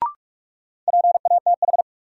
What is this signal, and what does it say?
Morse code, heard as a keyed tone, about 41 wpm, 700 Hz. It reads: PATH
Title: Sentences from Top 1400 Words - New Word Drill Alt - 45wpm